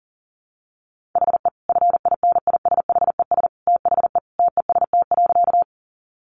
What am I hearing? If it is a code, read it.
Morse code, heard as a keyed tone, about 40 wpm, 700 Hz. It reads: HE FINISHES THE TEST.
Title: Additional Easy Phrases 40wpm